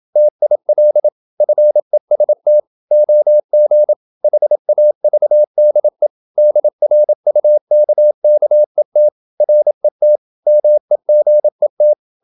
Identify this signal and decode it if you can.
Morse code, heard as a keyed tone, about 27 wpm, 600 Hz. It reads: TIL FEST OG HAVDE DRUKKET RET MEGET
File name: Ferie QRQ 27wpm